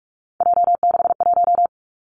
Morse code, heard as a keyed tone, about 46 wpm, 700 Hz. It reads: J61
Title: Three Challenging Letter-Number Combinations 45wpm